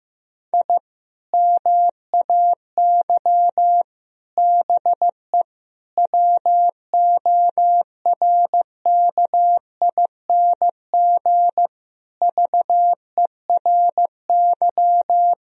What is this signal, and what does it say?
Morse code, heard as a keyed tone, about 15 wpm, 700 Hz. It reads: I MAY BE WORKING VERY
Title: Sentences from Top 1000 Words - Review Part 4 - 15wpm